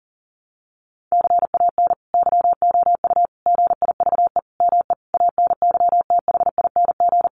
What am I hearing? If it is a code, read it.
Morse code, heard as a keyed tone, about 40 wpm, 700 Hz. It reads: CAN YOU GIVE ME ANYTHING
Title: Sentences from Top 1500 Words 40wpm